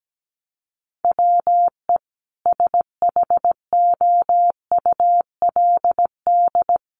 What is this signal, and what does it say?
Morse code, heard as a keyed tone, about 17 wpm, 700 Hz. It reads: WE SHOULD
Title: Sentences from Top 1400 Words 17wpm